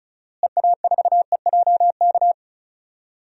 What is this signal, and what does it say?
Morse code, heard as a keyed tone, about 36 wpm, 700 Hz. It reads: EA4EJK
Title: DXCC-100 Call Signs - 6 Characters - Repeated 2x - 35wpm